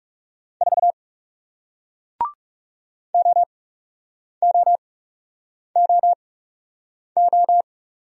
Morse code, heard as a keyed tone, about 40 wpm, spaced as for 8 wpm, 700 Hz. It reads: 4 OOOO